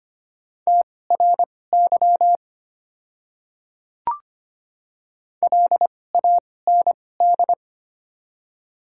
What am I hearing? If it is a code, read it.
Morse code, heard as a keyed tone, about 25 wpm, spaced as for 18 wpm, 700 Hz. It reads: TRY LAND
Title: 2x Character Spacing - Top 300 Words - 25wpm